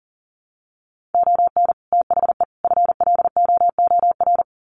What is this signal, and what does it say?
Morse code, heard as a keyed tone, about 40 wpm, 700 Hz. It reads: ON THE FLOOR
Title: Sets of 3 Words 40wpm